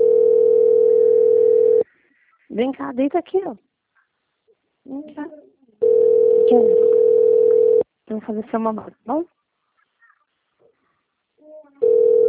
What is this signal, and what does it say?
Ringback tone (North America), 440 + 480 Hz, with speech.
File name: out- - - - - .4246